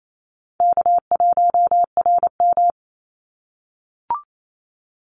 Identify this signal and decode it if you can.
Morse code, heard as a keyed tone, about 28 wpm, 700 Hz. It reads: K1RM
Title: Easy Call Signs - 28wpm